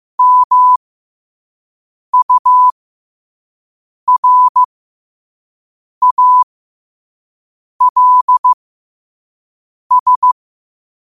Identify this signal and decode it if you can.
Morse code, heard as a keyed tone, about 15 wpm, spaced as for 5 wpm, 1000 Hz. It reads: MURALS
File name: qotdfiles / 5wpm